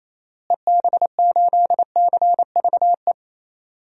Morse code, heard as a keyed tone, about 28 wpm, 700 Hz. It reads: EB8CVE
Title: DXCC-100 Call Signs - 6 Characters - Repeated 3x - 28wpm